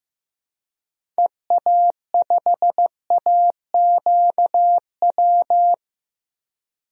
Morse code, heard as a keyed tone, about 15 wpm, 700 Hz. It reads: EA5AQW
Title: DXCC-100 Call Signs - 6 Characters - Repeated 3x - 15wpm